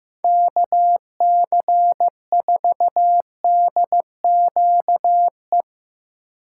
Morse code, heard as a keyed tone, about 16 wpm, 700 Hz. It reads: KC4DQE
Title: Speed Racing - USA 2x3 Call Signs 15wpm